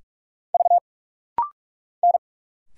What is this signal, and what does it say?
Morse code, heard as a keyed tone, about 46 wpm, 700 Hz. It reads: V N
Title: Single Letters - Rapid-Fire 45wpm